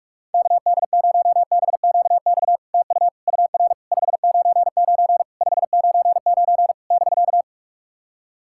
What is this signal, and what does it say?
Morse code, heard as a keyed tone, about 44 wpm, 700 Hz. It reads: KD0BQX TU UR 599 599 <BK>
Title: SOTA QSO Practice 45wpm